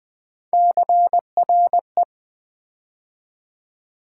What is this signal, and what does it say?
Morse code, heard as a keyed tone, about 20 wpm, 700 Hz. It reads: CRE
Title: Build A Word - 3 Letters At A Time - 6-Letter Words - 20wpm